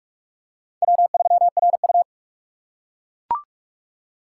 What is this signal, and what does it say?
Morse code, heard as a keyed tone, about 41 wpm, 700 Hz. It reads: W3RU
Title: USA Call Signs - 4 Characters - 4x Character Spacing - 45wpm